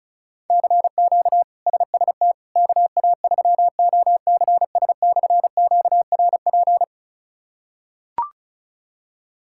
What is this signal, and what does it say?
Morse code, heard as a keyed tone, about 34 wpm, 700 Hz. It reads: CQ SST KA3OCS/QRP
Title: K1USN SST QSO Practice - 35wpm